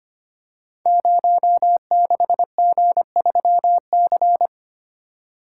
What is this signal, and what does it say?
Morse code, heard as a keyed tone, about 25 wpm, 700 Hz. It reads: 06G3C
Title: Code Groups - 5 Letter:Numbers - 25wpm